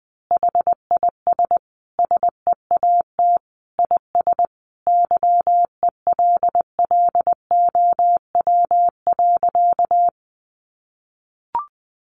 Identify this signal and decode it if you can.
Morse code, heard as a keyed tone, about 20 wpm, 700 Hz. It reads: HIS SEAT IS YELLOW.
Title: Additional Easy Phrases 20wpm